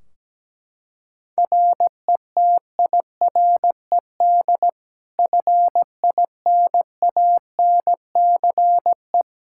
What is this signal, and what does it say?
Morse code, heard as a keyed tone, about 17 wpm, 700 Hz. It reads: RETIRED FINANCE
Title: K1USN SST QSO Practice 17wpm